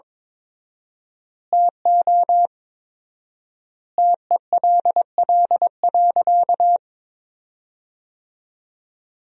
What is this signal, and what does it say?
Morse code, heard as a keyed tone, about 22 wpm, 700 Hz. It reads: TO TELL.
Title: 4x Word-Spacing - Sentences from Top 300 Words - 22wpm